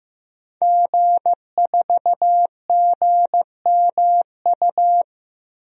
Morse code, heard as a keyed tone, about 15 wpm, 700 Hz. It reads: G4GMU
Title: DXCC-100 Call Signs - 5 Characters - Repeated 3x - 15wpm